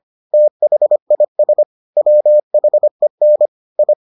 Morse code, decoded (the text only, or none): THIS WHEN I